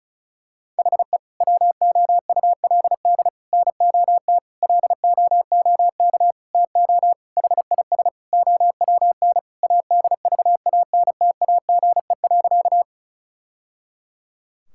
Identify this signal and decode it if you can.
Morse code, heard as a keyed tone, about 34 wpm, 700 Hz. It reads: HE WOULD NOT LOOK TO HIS OWN ADVANTAGE.
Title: Sentences from Top 1000 Words - Review Part 6 - 35wpm